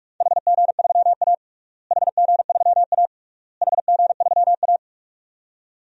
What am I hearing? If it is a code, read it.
Morse code, heard as a keyed tone, about 42 wpm, 700 Hz. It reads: HG3A HG3A HG3A
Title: DXCC-100 Call Signs - 4 Characters - Repeated 3x - 45wpm